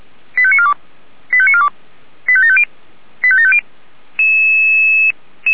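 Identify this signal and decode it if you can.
Five-tone selective call (ZVEI), heard as quick runs of five tones, tones of about 70 ms. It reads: 85942 85942 85750 85750
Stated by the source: Received by SDR.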